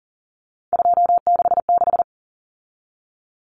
Morse code, heard as a keyed tone, about 40 wpm, 700 Hz. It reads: 266
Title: Common Cut Numbers - 3-Digit 40wpm